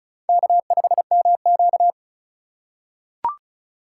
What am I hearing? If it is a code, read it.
Morse code, heard as a keyed tone, about 36 wpm, 700 Hz. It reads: K5MQ
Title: USA Call Signs - 3x Character Spacing - 35wpm